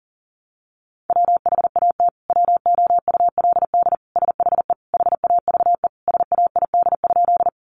Morse code, heard as a keyed tone, about 40 wpm, 700 Hz. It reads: WHAT WOULD SHE HAVE SAID?
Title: Sentences from Top 100 Words 40wpm